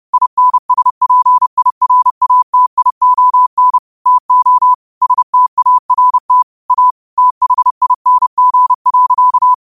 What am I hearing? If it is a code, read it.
Morse code, heard as a keyed tone, about 30 wpm, 1000 Hz. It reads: INSPIRATION TO START A THING.
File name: qotdfiles / 30wpm